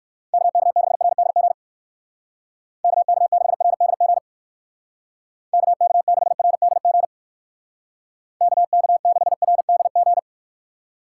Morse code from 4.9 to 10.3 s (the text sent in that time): KK6RDG KK6RDG